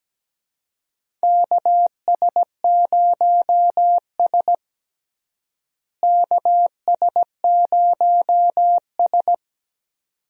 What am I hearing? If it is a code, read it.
Morse code, heard as a keyed tone, about 17 wpm, 700 Hz. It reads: KS0S KS0S
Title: USA Call Signs - 4 Characters - Repeated 2x - 17wpm